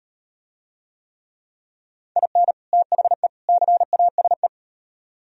Morse code, heard as a keyed tone, about 38 wpm, 700 Hz. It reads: IN THE CASE